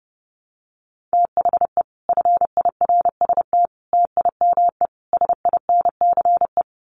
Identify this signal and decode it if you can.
Morse code, heard as a keyed tone, about 30 wpm, 700 Hz. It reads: THE FIRST TIME SINCE
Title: Sets of 4 Words 30wpm